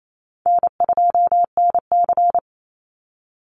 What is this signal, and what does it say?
Morse code, heard as a keyed tone, about 28 wpm, 700 Hz. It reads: N2NC
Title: Call Signs Repeated 3x - 28wpm